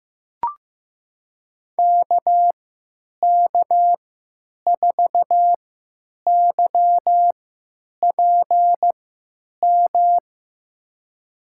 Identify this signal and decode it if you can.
Morse code, heard as a keyed tone, about 15 wpm, spaced as for 8 wpm, 700 Hz. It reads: KK4YPM
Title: USA Call Signs - 6 Characters - 3x Character Spacing - 15wpm